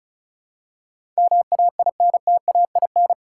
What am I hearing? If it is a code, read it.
Morse code, heard as a keyed tone, about 36 wpm, 700 Hz. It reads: MAINTAIN
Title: Sentences from Top 1000 Words - Review Part 5 - 35wpm